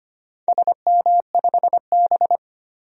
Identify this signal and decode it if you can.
Morse code, heard as a keyed tone, about 25 wpm, 700 Hz. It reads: SM5B
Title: DXCC-100 Call Signs - 4 Characters - 25wpm